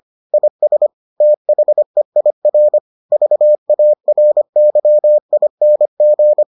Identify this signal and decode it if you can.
Morse code, heard as a keyed tone, about 25 wpm, 600 Hz. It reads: IS THEIR VARYING